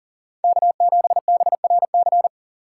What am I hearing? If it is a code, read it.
Morse code, heard as a keyed tone, about 40 wpm, 690 Hz. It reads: K7BRC
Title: USA Call Signs - 5 Characters - 3x Character Spacing - 40wpm